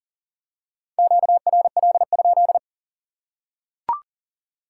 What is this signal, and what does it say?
Morse code, heard as a keyed tone, about 38 wpm, 700 Hz. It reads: QRL?
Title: Top 180 QSO Elements 40wpm